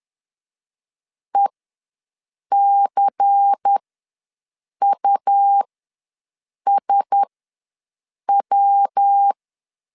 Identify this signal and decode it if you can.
Morse code, heard as a keyed tone, about 10 wpm, spaced as for 6 wpm, 800 Hz. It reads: ECUSW
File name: LEZIONI / 5 LBVC / 5 LBVC ws 10wpm 3733